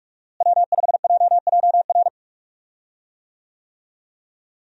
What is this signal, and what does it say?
Morse code, heard as a keyed tone, about 43 wpm, 700 Hz. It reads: W5JJR
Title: USA Call Signs - 5 Characters - Repeated 3x - 45wpm